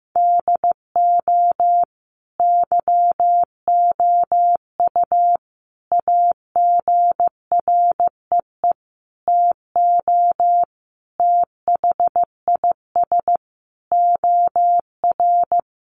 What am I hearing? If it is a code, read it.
Morse code, heard as a keyed tone, about 15 wpm, 700 Hz. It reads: DO YOU AGREE TO THIS OR
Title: Sentences from Top 1100 Words 15wpm